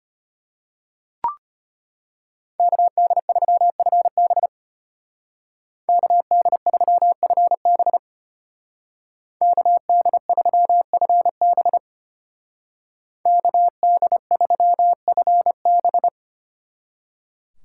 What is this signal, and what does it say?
Morse code, heard as a keyed tone, about 33 wpm, 700 Hz. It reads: KD3FB KD3FB KD3FB KD3FB